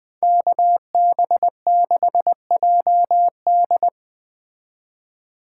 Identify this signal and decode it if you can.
Morse code, heard as a keyed tone, about 20 wpm, 700 Hz. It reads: KB6JD